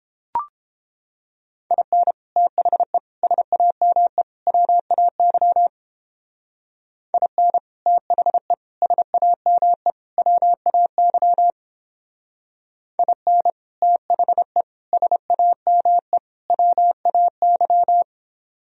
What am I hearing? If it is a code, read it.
Morse code, heard as a keyed tone, about 29 wpm, 700 Hz. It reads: IN THE SAME WAY IN THE SAME WAY IN THE SAME WAY